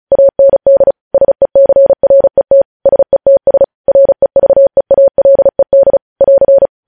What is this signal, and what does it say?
Morse code, heard as a keyed tone, about 36 wpm, 550 Hz. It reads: AND SECRET SETS REVEALED <AR>
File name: podcast / News headlines-1x-35x35